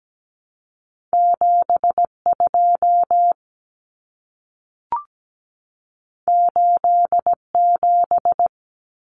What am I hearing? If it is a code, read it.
Morse code, heard as a keyed tone, about 17 wpm, 700 Hz. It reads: 72 87
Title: Numbers - 2-Digit 17wpm